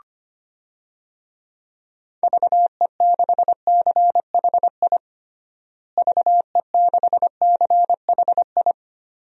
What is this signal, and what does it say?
Morse code, heard as a keyed tone, about 25 wpm, 700 Hz. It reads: VE6CHI VE6CHI
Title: DXCC-100 Call Signs - 6 Characters - Repeated 2x - 25wpm